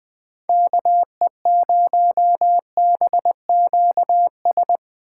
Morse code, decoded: KE0BQS

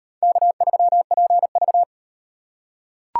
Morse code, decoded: K3PV

